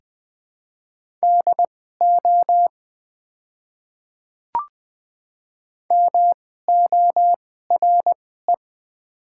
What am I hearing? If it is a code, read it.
Morse code, heard as a keyed tone, about 20 wpm, spaced as for 14 wpm, 700 Hz. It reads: DO MORE
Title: 2x Character Spacing - Top 100 Words - 20wpm